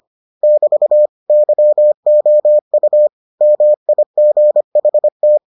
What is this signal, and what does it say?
Morse code, heard as a keyed tone, about 25 wpm, 600 Hz. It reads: <BT> YOU MIGHT